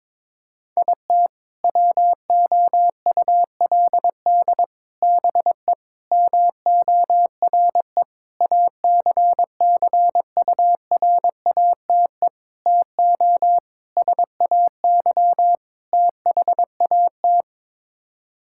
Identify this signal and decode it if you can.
Morse code, heard as a keyed tone, about 22 wpm, 700 Hz. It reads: IT WOULD BE MORE ACCURATE TO SAY THAT
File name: practice / 22wpm / Sets of 8 Words - 22wpm